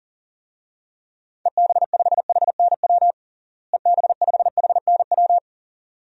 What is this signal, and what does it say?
Morse code, heard as a keyed tone, about 40 wpm, 700 Hz. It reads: EB5HNW EB5HNW
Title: DXCC-100 Call Signs - 6 Characters - Repeated 2x - 40wpm